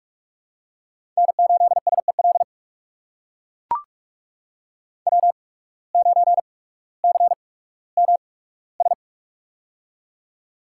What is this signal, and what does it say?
Morse code, heard as a keyed tone, about 41 wpm, spaced as for 19 wpm, 700 Hz. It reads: N8SEL W9CMS
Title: USA Call Signs - 5 Characters - 8x Character Spacing - 45wpm